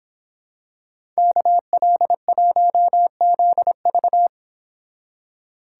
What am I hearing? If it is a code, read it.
Morse code, heard as a keyed tone, about 26 wpm, 700 Hz. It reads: KL1ZV